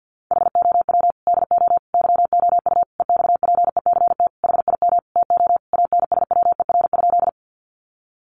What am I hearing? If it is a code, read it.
Morse code, heard as a keyed tone, about 50 wpm, 700 Hz. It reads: HOW DO YOU EXPECT HIM TO ANSWER?
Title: Sentences from Top 1100 Words 50wpm